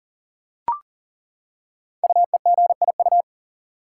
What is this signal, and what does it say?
Morse code, heard as a keyed tone, about 40 wpm, 700 Hz. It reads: UEGIU
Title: Code Groups - 5 Letters - 40wpm